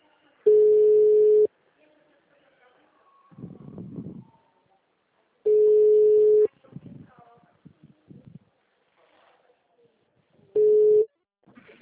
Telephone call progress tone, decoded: ringback tone

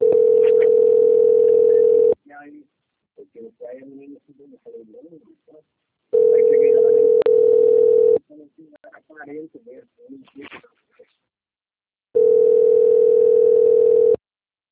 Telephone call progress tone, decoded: ringback tone

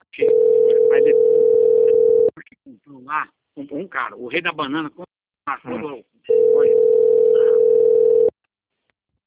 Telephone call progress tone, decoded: ringback tone

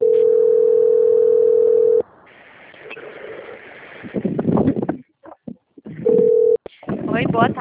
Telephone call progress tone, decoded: ringback tone

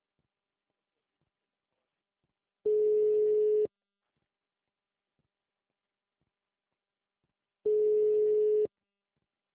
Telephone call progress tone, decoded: ringback tone

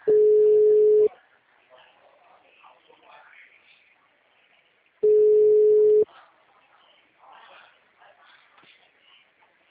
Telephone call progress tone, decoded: ringback tone

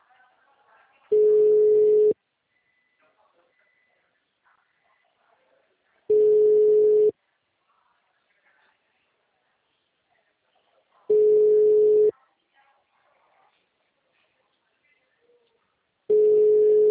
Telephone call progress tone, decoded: ringback tone

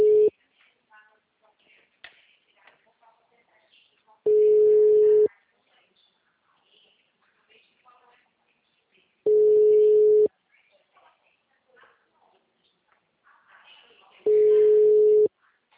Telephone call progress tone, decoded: ringback tone

